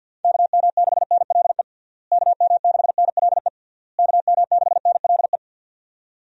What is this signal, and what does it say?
Morse code, heard as a keyed tone, about 52 wpm, 700 Hz. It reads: KM6NLE KM6NLE KM6NLE
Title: USA Call Signs - 6 Characters - Repeated 3x - 50wpm